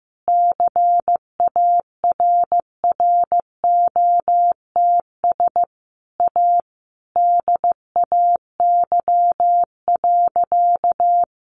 Morse code, heard as a keyed tone, about 15 wpm, 700 Hz. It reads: CARROTS A DAY.
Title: Easy Phrases 15wpm